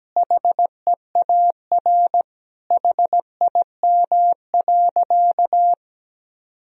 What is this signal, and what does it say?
Morse code, heard as a keyed tone, about 17 wpm, 700 Hz. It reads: HEAR HIM.
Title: Sentences from Top 1000 Words - Review Part 1 - 17wpm